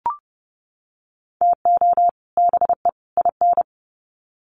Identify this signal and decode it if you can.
Morse code, heard as a keyed tone, about 29 wpm, 700 Hz. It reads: TO BE IN
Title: Sets of 3 Words 30wpm